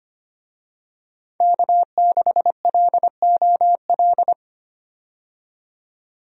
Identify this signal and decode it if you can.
Morse code, heard as a keyed tone, about 25 wpm, 700 Hz. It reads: K6LOL
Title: USA Call Signs - 5 Characters - 25wpm